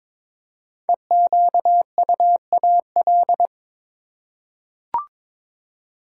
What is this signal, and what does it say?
Morse code, heard as a keyed tone, about 22 wpm, 700 Hz. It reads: EQUAL
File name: 5 Letter Words - First 3 Letters Given - 22wpm